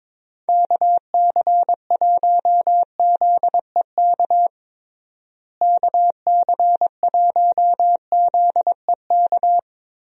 Morse code, heard as a keyed tone, about 22 wpm, 700 Hz. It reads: KC1ZEK KC1ZEK
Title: USA Call Signs - 6 Characters - Repeated 3x - 22wpm